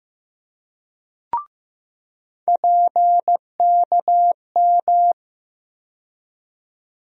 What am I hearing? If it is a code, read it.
Morse code, heard as a keyed tone, about 15 wpm, 700 Hz. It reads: PKM
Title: USA Call Sign Postfixes - 3 Letters 15wpm